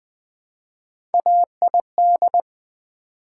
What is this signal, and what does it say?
Morse code, heard as a keyed tone, about 20 wpm, 700 Hz. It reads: AID
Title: Build A Word - 1 Letter At A Time - 3-Letter Words - 20wpm